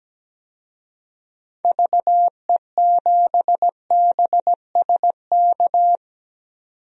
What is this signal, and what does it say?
Morse code, heard as a keyed tone, about 17 wpm, 700 Hz. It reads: VE7BSK
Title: DXCC-100 Call Signs - 6 Characters - 17wpm